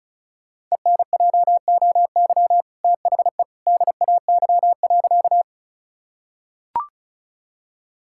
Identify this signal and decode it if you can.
Morse code, heard as a keyed tone, about 34 wpm, 700 Hz. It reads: ENJOY THE DAY.
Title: Sentences from Top 1000 Words - Review Part 5 - 35wpm